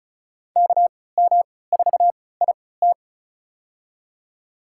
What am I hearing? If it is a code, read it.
Morse code, heard as a keyed tone, about 34 wpm, spaced as for 20 wpm, 700 Hz. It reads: KM4IT
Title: USA Call Signs - 5 Characters - 3x Character Spacing - 35wpm